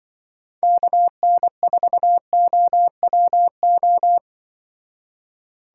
Morse code, heard as a keyed tone, about 23 wpm, 700 Hz. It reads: KN4OWO
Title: Speed Racing - USA 2x3 Call Signs 20wpm